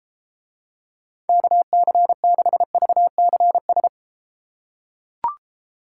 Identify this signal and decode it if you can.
Morse code, heard as a keyed tone, about 31 wpm, 700 Hz. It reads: KC6VCS